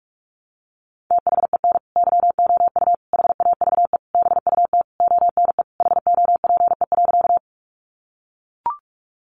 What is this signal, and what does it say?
Morse code, heard as a keyed tone, about 44 wpm, 700 Hz. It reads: THEN YOU HAVE BUT ONE HOPE.
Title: Sentences from Top 1000 Words 45wpm